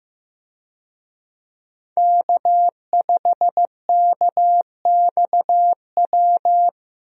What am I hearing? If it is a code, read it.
Morse code, heard as a keyed tone, about 15 wpm, 700 Hz. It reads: K5KXW
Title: USA Call Signs - 5 Characters - Repeated 3x - 15wpm